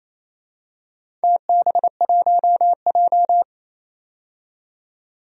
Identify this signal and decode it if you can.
Morse code, heard as a keyed tone, about 28 wpm, 700 Hz. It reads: TB1J
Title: DXCC-100 Call Signs - 4 Characters - 28wpm